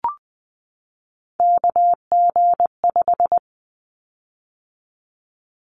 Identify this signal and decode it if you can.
Morse code, heard as a keyed tone, about 20 wpm, 700 Hz. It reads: KG5
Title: Callsign Prefixes for Top 25 DXCC - 20wpm